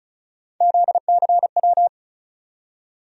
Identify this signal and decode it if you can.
Morse code, heard as a keyed tone, about 36 wpm, 700 Hz. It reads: ZCW
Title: USA Call Sign Postfixes - 3 Letters 35wpm